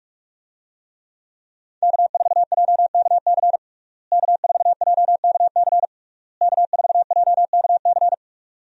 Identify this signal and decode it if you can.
Morse code, heard as a keyed tone, about 41 wpm, 700 Hz. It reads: K4JKC K4JKC K4JKC
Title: USA Call Signs - 5 Characters - Repeated 3x - 45wpm